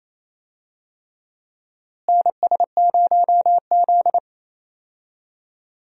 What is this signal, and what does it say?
Morse code, heard as a keyed tone, about 28 wpm, 700 Hz. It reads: NS0Z
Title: USA Call Signs - 4 Characters - Repeated 3x - 28wpm